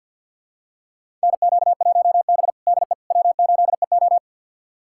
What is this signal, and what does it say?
Morse code, heard as a keyed tone, about 49 wpm, 700 Hz. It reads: NQ1B DE W8EO
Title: USA Callsign DE Callsign - 4 Characters - 50wpm